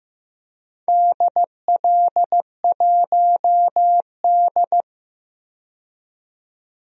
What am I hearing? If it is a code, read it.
Morse code, heard as a keyed tone, about 15 wpm, 700 Hz. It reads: DL1D
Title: DXCC-100 Call Signs - 4 Characters - 15wpm